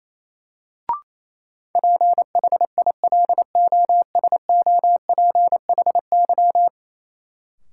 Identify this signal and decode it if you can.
Morse code, heard as a keyed tone, about 28 wpm, 700 Hz. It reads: PHILOSOPHY
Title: Sentences from Top 900 Words - New Word Drill - 28wpm